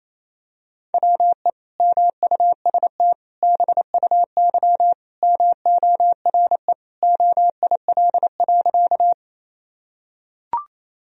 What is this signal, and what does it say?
Morse code, heard as a keyed tone, about 28 wpm, 700 Hz. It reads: WE MUST BUY MORE OIL.
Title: Sentences from Top 1000 Words - Review Part 4 - 28wpm